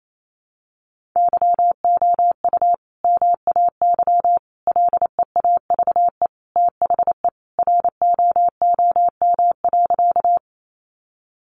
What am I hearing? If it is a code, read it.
Morse code, heard as a keyed tone, about 28 wpm, 700 Hz. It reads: YOU MAY LEAVE THE ROOM.
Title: Sentences from Top 400 Words 28wpm